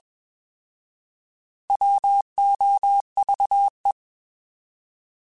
Morse code, heard as a keyed tone, about 21 wpm, 800 Hz. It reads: WOVE
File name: Alphabetical / Alphabetical mp3s / Ws @20